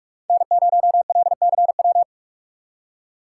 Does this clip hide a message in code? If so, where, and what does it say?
Yes: Morse code from 0.3 to 2.0 s, heard as a keyed tone, about 44 wpm, 700 Hz. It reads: N0LCW